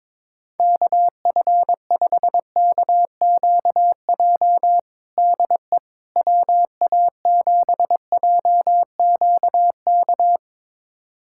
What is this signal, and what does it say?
Morse code, heard as a keyed tone, about 22 wpm, 700 Hz. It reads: KF5KQJ DE WA7JQK
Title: USA Callsign DE Callsign 22wpm